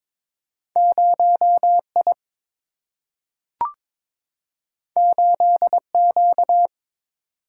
Morse code, heard as a keyed tone, about 22 wpm, 700 Hz. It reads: 0 I 8 Q
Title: Single-Letter Single-Number 22wpm